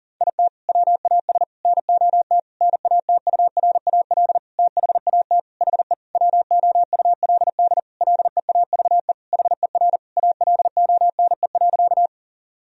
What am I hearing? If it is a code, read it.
Morse code, heard as a keyed tone, about 40 wpm, 700 Hz. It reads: IT WAS NOT NATURAL THAT HE WOULD LEAVE HER ALONE.
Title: Sentences from Top 1000 Words - Review Part 4 - 40wpm